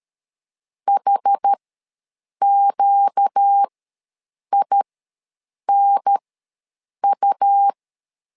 Morse code, heard as a keyed tone, about 13 wpm, spaced as for 7 wpm, 800 Hz. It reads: HQINU
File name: LEZIONI / 4 Q1 ws 12wpm 2129